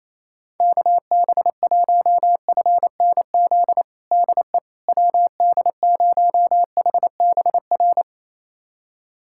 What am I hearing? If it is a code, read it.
Morse code, heard as a keyed tone, about 28 wpm, 700 Hz. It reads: KB1FNZ DE WD0HBR
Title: USA Callsign DE Callsign - 6 Characters - 28wpm